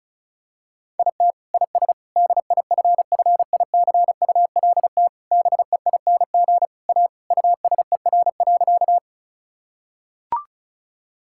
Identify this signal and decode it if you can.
Morse code, heard as a keyed tone, about 34 wpm, 700 Hz. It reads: IT IS DIFFICULT BEING A USER.
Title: Sentences from Top 1000 Words - Review Part 5 - 35wpm